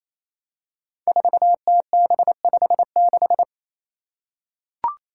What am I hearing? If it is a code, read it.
Morse code, heard as a keyed tone, about 28 wpm, 700 Hz. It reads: 4TB56
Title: Five Challenging Letter-Number Combinations 28wpm